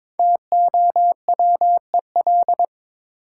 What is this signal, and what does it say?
Morse code, heard as a keyed tone, about 22 wpm, 700 Hz. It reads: TOWEL
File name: 5 Letter Words - First 3 Letters Given - 22wpm